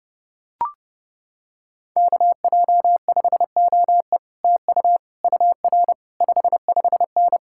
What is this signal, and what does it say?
Morse code, heard as a keyed tone, about 29 wpm, 700 Hz. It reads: KJ5OE TU UR 55N